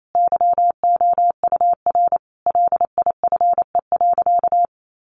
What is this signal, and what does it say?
Morse code, heard as a keyed tone, about 28 wpm, 700 Hz. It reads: YOUR LIFE.
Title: Sentences from Top 700 Words 28wpm